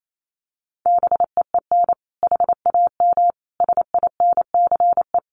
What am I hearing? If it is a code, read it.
Morse code, heard as a keyed tone, about 28 wpm, 700 Hz. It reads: BEEN HAM SINCE
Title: Common QSO Phrases 28wpm